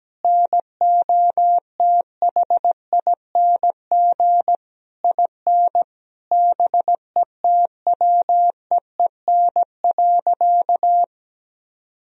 Morse code, heard as a keyed tone, about 17 wpm, 700 Hz. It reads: NOTHING IN BETWEEN.
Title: Sentences from Top 1000 Words - Review Part 1 - 17wpm